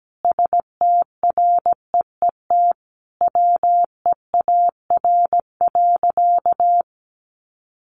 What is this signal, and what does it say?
Morse code, heard as a keyed tone, about 17 wpm, 700 Hz. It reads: STREET WEAR.
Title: Sentences from Top 1500 Words 17wpm